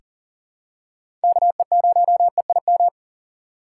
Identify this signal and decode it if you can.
Morse code, heard as a keyed tone, about 40 wpm, 700 Hz. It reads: KE0EIM
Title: USA Call Signs - 6 Characters - 2x Character Spacing - 40wpm